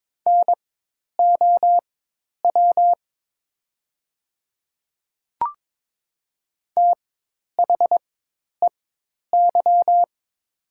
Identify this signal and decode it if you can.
Morse code, heard as a keyed tone, about 22 wpm, spaced as for 10 wpm, 700 Hz. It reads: NOW THEY
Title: 4x Character Spacing - Top 100 Words - 22wpm